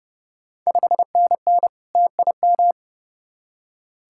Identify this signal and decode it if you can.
Morse code, heard as a keyed tone, about 31 wpm, 700 Hz. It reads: 5NN TIM
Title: Ontario QSO Party Exchange 30wpm